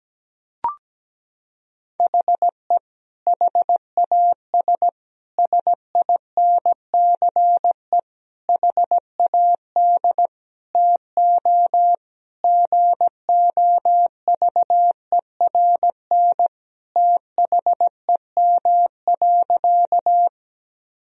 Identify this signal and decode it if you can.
Morse code, heard as a keyed tone, about 17 wpm, 700 Hz. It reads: HE HAS SINCE HAD TO GOVERN THEM.
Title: Sentences from Top 1000 Words - Review Part 2 - 17wpm